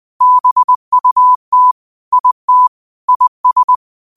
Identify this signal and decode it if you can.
Morse code, heard as a keyed tone, about 20 wpm, 1000 Hz. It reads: BUT IT IS
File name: qotdfiles / 20wpm